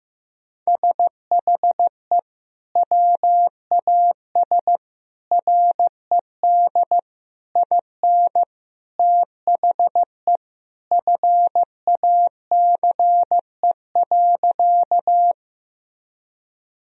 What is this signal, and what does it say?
Morse code, heard as a keyed tone, about 15 wpm, 700 Hz. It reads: SHE WAS RED IN THE FACE.
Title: Sentences from Top 1000 Words - Review Part 2 - 15wpm